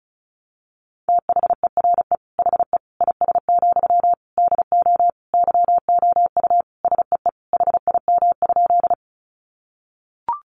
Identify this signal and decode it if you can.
Morse code, heard as a keyed tone, about 34 wpm, 700 Hz. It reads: THERE HE IS, DO YOU SEE HIM?
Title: Sentences from Top 100 Words 35wpm